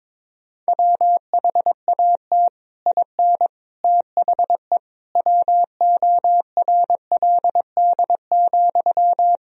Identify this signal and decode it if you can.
Morse code, heard as a keyed tone, about 22 wpm, 700 Hz. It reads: WHAT IN THE WORLD,
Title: Sentences from Top 1000 Words - Review Part 1 - 22wpm